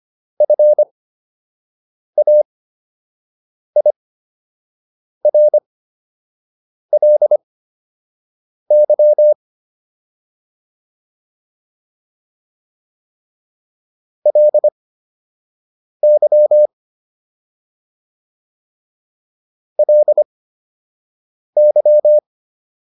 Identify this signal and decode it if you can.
Morse code, heard as a keyed tone, about 25 wpm, spaced as for 6 wpm, 600 Hz. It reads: FAIRLY LY LY